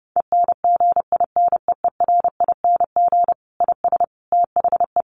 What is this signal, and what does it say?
Morse code, heard as a keyed tone, about 30 wpm, 700 Hz. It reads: ENGINEERING IS THE
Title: Sentences from Top 1500 Words 30wpm